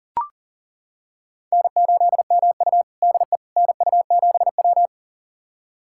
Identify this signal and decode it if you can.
Morse code, heard as a keyed tone, about 40 wpm, 700 Hz. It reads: N8MU DE NU7W
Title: USA Callsign DE Callsign - 4 Characters - 40wpm